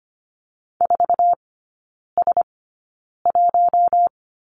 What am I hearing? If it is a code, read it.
Morse code, heard as a keyed tone, about 25 wpm, spaced as for 9 wpm, 700 Hz. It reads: 4S1
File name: Lesson 14e - Number 2 - Three Characters 25wpm